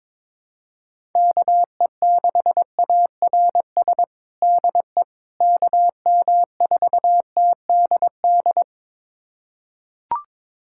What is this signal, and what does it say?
Morse code, heard as a keyed tone, about 22 wpm, 700 Hz. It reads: KE6ARS DE KM4TDD
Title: USA Callsign DE Callsign - 6 Characters - 22wpm